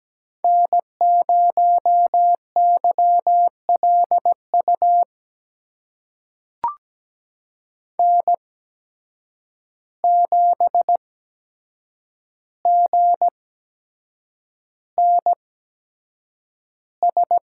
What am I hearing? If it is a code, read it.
Morse code, heard as a keyed tone, about 17 wpm, spaced as for 7 wpm, 700 Hz. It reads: N0YLU N7GNS
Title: USA Call Signs - 5 Characters - 8x Character Spacing - 17wpm